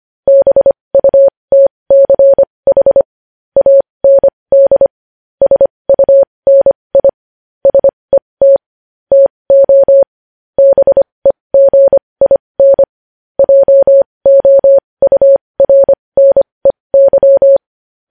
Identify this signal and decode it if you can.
Morse code, heard as a keyed tone, about 25 wpm, spaced as for 20 wpm, 550 Hz. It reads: BUTCH AND SUNI SET TO BEGIN JOURNEY
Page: news-headlines-in-morse-code-at-20-wpm